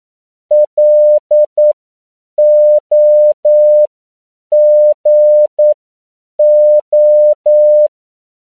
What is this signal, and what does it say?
Morse code, heard as a keyed tone, about 9 wpm, spaced as for 7 wpm, 600 Hz. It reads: LOGO